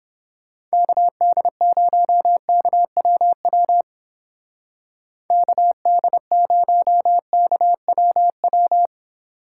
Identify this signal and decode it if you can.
Morse code, heard as a keyed tone, about 30 wpm, 700 Hz. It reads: KD0KWW KD0KWW